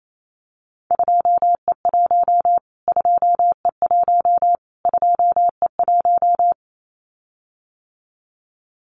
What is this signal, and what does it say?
Morse code, heard as a keyed tone, about 28 wpm, 700 Hz. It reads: 2E1 2E1 2E1
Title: Callsign Prefixes for Top 25 DXCC - Repeated 3x - 28wpm